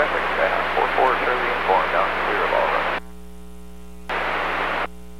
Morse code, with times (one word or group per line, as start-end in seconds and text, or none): none